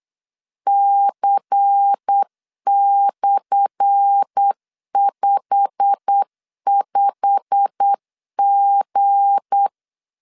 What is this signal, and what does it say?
Morse code, heard as a keyed tone, about 8 wpm, 800 Hz. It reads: C/55G
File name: LEZIONI / 10 SIGNS / 10A SLASH 8wpm 3452